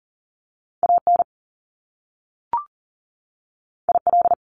Morse code, heard as a keyed tone, about 39 wpm, 700 Hz. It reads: AN IL
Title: Top Two Letter Combinations in Words 40wpm